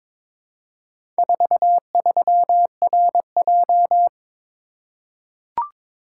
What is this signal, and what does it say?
Morse code, heard as a keyed tone, about 22 wpm, 700 Hz. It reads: 43RJ
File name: practice / 22wpm / Four Letter-Number Combinations - 22wpm